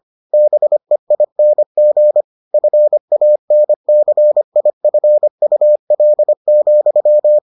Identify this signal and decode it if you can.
Morse code, heard as a keyed tone, about 25 wpm, 600 Hz. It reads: BEING FANCIFUL,